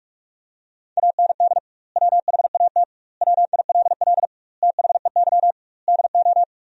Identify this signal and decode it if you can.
Morse code, heard as a keyed tone, about 43 wpm, 700 Hz. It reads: AND WHAT WILL THEY DO
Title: 2x Word-Spacing - Sentences from Top 200 Words - 45wpm